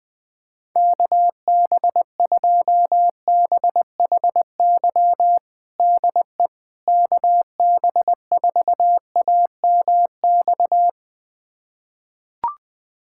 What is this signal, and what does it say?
Morse code, heard as a keyed tone, about 21 wpm, 700 Hz. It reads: KB2BHY DE KB4AMX
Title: USA Callsign DE Callsign - 6 Characters - 20wpm